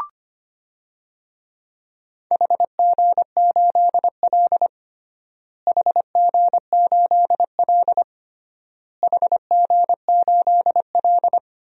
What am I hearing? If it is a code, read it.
Morse code, heard as a keyed tone, about 25 wpm, 700 Hz. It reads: HG8L HG8L HG8L
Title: DXCC-100 Call Signs - 4 Characters - Repeated 3x - 25wpm